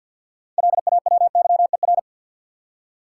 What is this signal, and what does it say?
Morse code, heard as a keyed tone, about 52 wpm, 700 Hz. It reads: LAWYER
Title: Six Letter Words - 50wpm